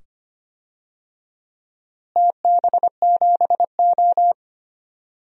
Morse code, heard as a keyed tone, about 25 wpm, 700 Hz. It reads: TB7O
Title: DXCC-100 Call Signs - 4 Characters - Repeated 2x - 25wpm